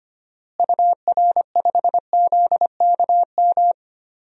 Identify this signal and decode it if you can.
Morse code, heard as a keyed tone, about 25 wpm, 700 Hz. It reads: UR5ZKM
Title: DXCC-100 Call Signs - 6 Characters - Repeated 3x - 25wpm